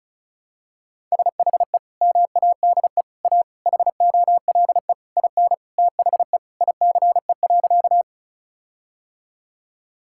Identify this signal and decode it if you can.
Morse code, heard as a keyed tone, about 36 wpm, 700 Hz. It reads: SHE MADE A HOLE IN THE ICE.
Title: Sentences from Top 1600 Words 35wpm